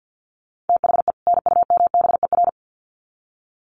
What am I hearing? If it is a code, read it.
Morse code, heard as a keyed tone, about 50 wpm, 700 Hz. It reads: THE NUMBER
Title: Sets of 2 Words 50wpm